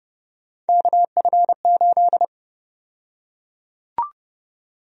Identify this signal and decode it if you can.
Morse code, heard as a keyed tone, about 30 wpm, 700 Hz. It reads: KF8
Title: USA Call Sign Prefixes - Repeated 2x - 30wpm